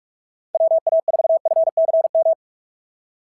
Morse code, heard as a keyed tone, about 44 wpm, 690 Hz. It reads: WA4FCM
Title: SOTA QSO Practice 45wpm